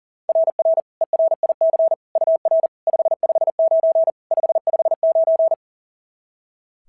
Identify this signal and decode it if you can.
Morse code, heard as a keyed tone, about 40 wpm, 690 Hz. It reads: RR ERIC UR 559 559
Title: SOTA QSO Practice 40wpm